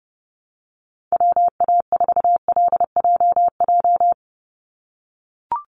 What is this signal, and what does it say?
Morse code, heard as a keyed tone, about 30 wpm, 700 Hz. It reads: WA4LJJ
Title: Call Signs - 30wpm